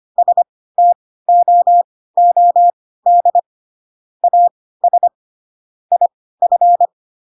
Morse code, heard as a keyed tone, about 25 wpm, spaced as for 16 wpm, 700 Hz. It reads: STOOD AS IF